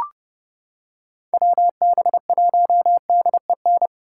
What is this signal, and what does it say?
Morse code, heard as a keyed tone, about 30 wpm, 700 Hz. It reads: WB1DEN